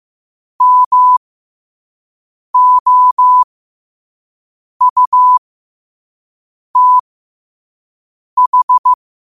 Morse code, heard as a keyed tone, about 15 wpm, spaced as for 5 wpm, 1000 Hz. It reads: MOUTH
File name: qotdfiles / 5wpm